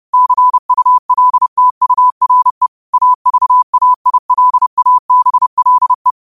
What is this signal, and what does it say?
Morse code, heard as a keyed tone, about 30 wpm, 1000 Hz. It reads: CULTURE AVAILABLE